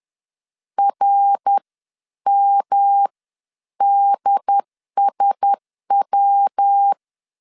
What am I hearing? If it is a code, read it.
Morse code, heard as a keyed tone, about 10 wpm, spaced as for 9 wpm, 800 Hz. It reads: RMDSW
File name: CW / DET / 1 6 1500C S C 10wpm 4252